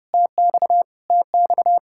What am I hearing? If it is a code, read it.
Morse code, heard as a keyed tone, about 30 wpm, 700 Hz. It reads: TX TX
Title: Practice QSO - Exchange 1 - Beginning - 30wpm